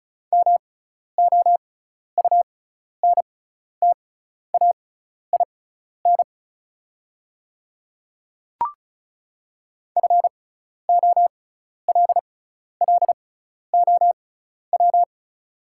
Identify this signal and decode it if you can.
Morse code, heard as a keyed tone, about 34 wpm, spaced as for 12 wpm, 700 Hz. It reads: MOUNTAIN FOLLOW
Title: 6x Character Spacing - Top 500 Words - 35wpm